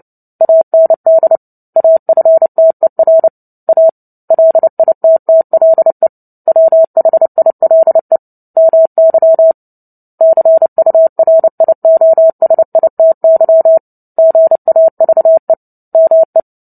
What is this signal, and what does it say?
Morse code, heard as a keyed tone, about 30 wpm, 650 Hz. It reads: AND AFTER A LITTLE WHILE MY CURIOSITY GAVE ME
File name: Book Two / 03-Chapter 3 - 29 WPM